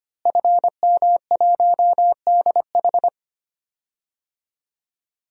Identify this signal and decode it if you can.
Morse code, heard as a keyed tone, about 25 wpm, 700 Hz. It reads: FM1DH
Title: DXCC-100 Call Signs - 5 Characters - Repeated 2x - 25wpm